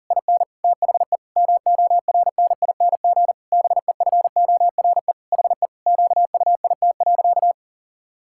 Morse code, heard as a keyed tone, about 40 wpm, 700 Hz. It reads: IN THE MORNING BEFORE HE QUIT.
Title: Pangram Challenge v2 - 40wpm